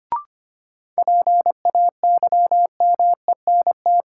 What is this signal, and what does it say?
Morse code, heard as a keyed tone, about 25 wpm, 700 Hz. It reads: PAYMENT
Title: Sentences from Top 1200 Words - New Word Drill - 25wpm